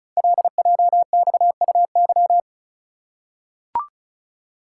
Morse code, heard as a keyed tone, about 36 wpm, 700 Hz. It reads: LJXUY
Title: Code Groups - 5 Letter:Numbers - 35wpm